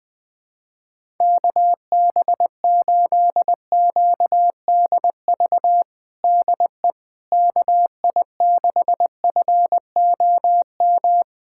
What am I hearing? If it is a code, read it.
Morse code, heard as a keyed tone, about 20 wpm, 700 Hz. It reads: KB8QDV DE KI6FOM
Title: USA Callsign DE Callsign - 6 Characters - 20wpm